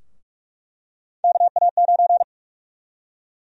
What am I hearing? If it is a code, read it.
Morse code, heard as a keyed tone, about 42 wpm, 700 Hz. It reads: KA9
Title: USA Call Sign Prefixes - Repeated 2x - 45wpm